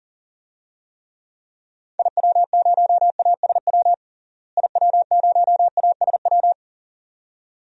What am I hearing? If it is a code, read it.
Morse code, heard as a keyed tone, about 40 wpm, 700 Hz. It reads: IW0ASW IW0ASW
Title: DXCC-100 Call Signs - 6 Characters - Repeated 2x - 40wpm